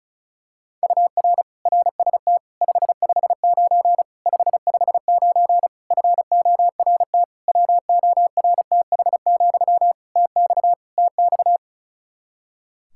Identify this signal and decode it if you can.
Morse code, heard as a keyed tone, about 34 wpm, 700 Hz. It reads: UR RST 559 559 FORT WORTH, TX TX
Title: Practice QSO - Exchange 1 - Beginning - 35wpm